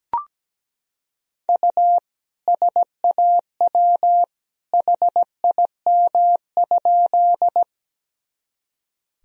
Morse code, heard as a keyed tone, about 17 wpm, 700 Hz. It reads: U SAW HIM?
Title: Sentences from Top 300 Words and Abbreviations 17wpm